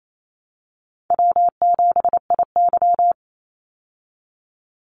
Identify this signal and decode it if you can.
Morse code, heard as a keyed tone, about 28 wpm, 700 Hz. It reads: W7IY
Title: Easy Call Signs Repeated 3x - 28wpm